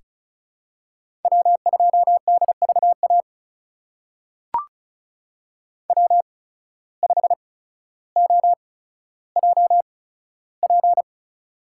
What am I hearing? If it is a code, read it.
Morse code, heard as a keyed tone, about 36 wpm, spaced as for 15 wpm, 700 Hz. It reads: W2DVA W5OJP
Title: USA Call Signs - 5 Characters - 8x Character Spacing - 35wpm